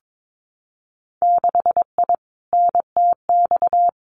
Morse code, heard as a keyed tone, about 22 wpm, 700 Hz. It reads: 6I NTX
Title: Winter Field Day Exchanges 22wpm